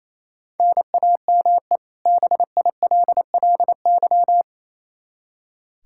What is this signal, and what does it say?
Morse code, heard as a keyed tone, about 28 wpm, 700 Hz. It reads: NAME BILLY
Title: Practice QSO - Exchange 1 - Beginning - 28wpm